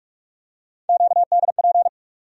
Morse code, heard as a keyed tone, about 46 wpm, 700 Hz. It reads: QDP
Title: USA Call Sign Postfixes - 3 Letters 45wpm